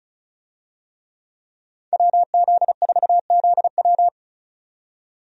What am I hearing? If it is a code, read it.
Morse code, heard as a keyed tone, about 36 wpm, 700 Hz. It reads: WZ4ZW
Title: USA Call Signs - 5 Characters - 35wpm